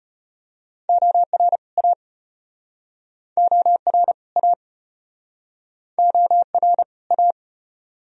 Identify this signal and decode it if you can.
Morse code, heard as a keyed tone, about 38 wpm, 700 Hz. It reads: OR A OR A OR A